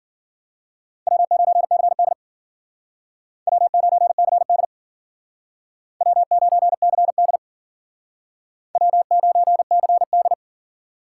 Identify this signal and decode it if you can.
Morse code, heard as a keyed tone, about 52 wpm, 700 Hz. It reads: W9CD W9CD W9CD W9CD